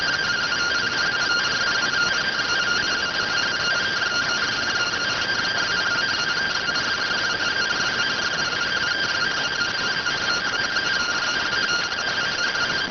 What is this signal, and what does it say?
Two-tone teleprinter signal, about 170 Hz shift.